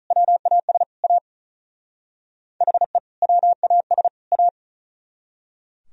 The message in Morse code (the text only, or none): WAS A HE WAS A